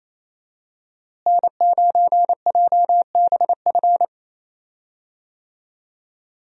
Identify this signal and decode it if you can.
Morse code, heard as a keyed tone, about 28 wpm, 700 Hz. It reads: N9JBF
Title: USA Call Signs - 5 Characters - Repeated 2x - 28wpm